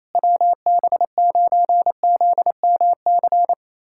Morse code, heard as a keyed tone, about 28 wpm, 700 Hz. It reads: WB9ZMC
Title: USA Callsign DE Callsign 28wpm